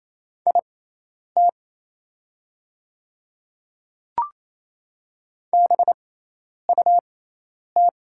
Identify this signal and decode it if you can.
Morse code, heard as a keyed tone, about 28 wpm, spaced as for 10 wpm, 700 Hz. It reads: IT BUT
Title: 6x Character Spacing - Top 300 Words - 28wpm